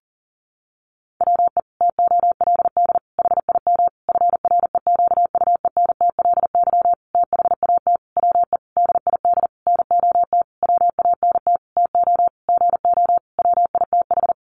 Morse code, heard as a keyed tone, about 40 wpm, 700 Hz. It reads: WE TOLD HIM FREQUENTLY THAT WE DID NOT WANT TO GO WITH
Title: Sentences from Top 1300 Words 40wpm